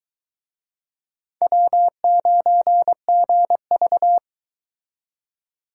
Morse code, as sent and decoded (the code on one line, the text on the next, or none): .-- ----. --. ...-
W9GV